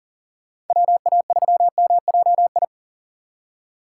Morse code, heard as a keyed tone, about 52 wpm, 700 Hz. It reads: WA3MJI